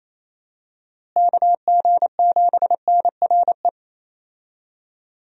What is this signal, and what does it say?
Morse code, heard as a keyed tone, about 28 wpm, 700 Hz. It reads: KG7NRE
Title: USA Call Signs - 6 Characters - Repeated 3x - 28wpm